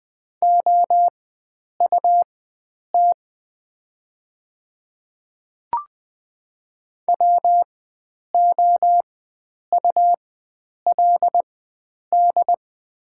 Morse code, heard as a keyed tone, about 20 wpm, spaced as for 9 wpm, 690 Hz. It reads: OUT WOULD
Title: 4x Character Spacing - Top 100 Words - 20wpm